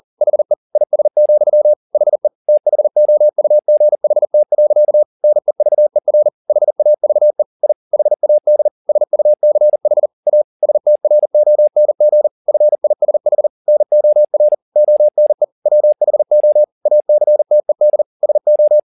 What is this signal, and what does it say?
Morse code, heard as a keyed tone, about 40 wpm, 600 Hz. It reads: HE IS, HE THOUGHT. NEVER HAVE I HAD SUCH A STRONG FISH NOR ONE WHO ACTED SO